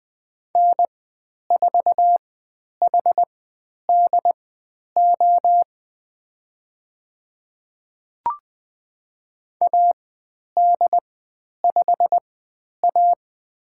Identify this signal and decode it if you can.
Morse code, heard as a keyed tone, about 20 wpm, spaced as for 10 wpm, 700 Hz. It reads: N4HDO AD5A
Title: 102 Lesson 17g - Letter u - Call Signs 20wpm